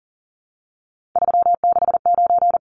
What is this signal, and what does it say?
Morse code, heard as a keyed tone, about 40 wpm, 700 Hz. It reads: 369
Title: Numbers - 3-Digit 40wpm